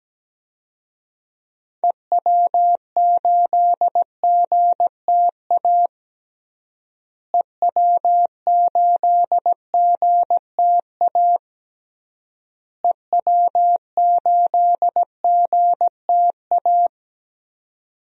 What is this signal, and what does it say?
Morse code, heard as a keyed tone, about 17 wpm, 700 Hz. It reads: EW8GTA EW8GTA EW8GTA
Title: DXCC-100 Call Signs - 6 Characters - Repeated 3x - 17wpm